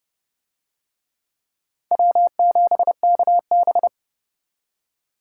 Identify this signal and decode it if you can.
Morse code, heard as a keyed tone, about 26 wpm, 700 Hz. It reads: W7KB